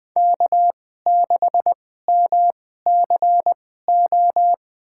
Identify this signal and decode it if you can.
Morse code, heard as a keyed tone, about 20 wpm, spaced as for 14 wpm, 700 Hz. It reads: K6MCO
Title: USA Call Signs - 2x Character Spacing - 20wpm